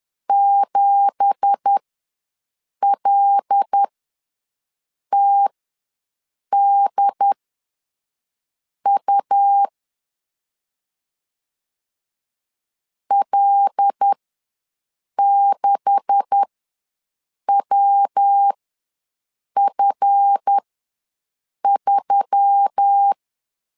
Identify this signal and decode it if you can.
Morse code, heard as a keyed tone, about 10 wpm, spaced as for 6 wpm, 800 Hz. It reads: 7LTDU L6WF3
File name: LEZIONI / 9 2867 ws 10wpm 4215